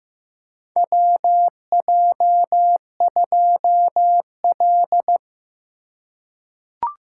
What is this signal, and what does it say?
Morse code, heard as a keyed tone, about 15 wpm, 700 Hz. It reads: WJ2L
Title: K1USN SST QSO Practice - 15wpm